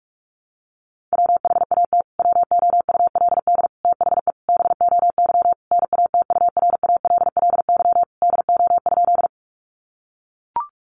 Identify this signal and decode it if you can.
Morse code, heard as a keyed tone, about 44 wpm, 700 Hz. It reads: WHAT WOULD THE BOY NATURALLY DO?
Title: Sentences from Top 1300 Words 45wpm